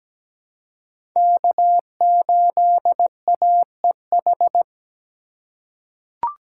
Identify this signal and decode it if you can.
Morse code, heard as a keyed tone, about 17 wpm, 700 Hz. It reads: K8AEH
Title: USA Call Signs - 5 Characters - 8x Character Spacing - 17wpm